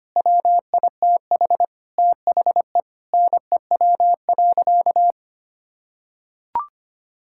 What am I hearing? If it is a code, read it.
Morse code, heard as a keyed tone, about 25 wpm, 700 Hz. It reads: WITH THE NEW.
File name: practice / 25wpm / Sentences with 9-Word Phrases - 25wpm